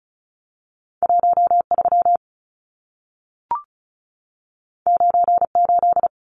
Morse code, heard as a keyed tone, about 36 wpm, 700 Hz. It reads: 13 98
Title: Numbers - 2-Digit 35wpm